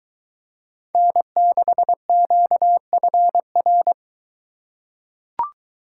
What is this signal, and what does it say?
Morse code, heard as a keyed tone, about 23 wpm, 700 Hz. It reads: N6QFR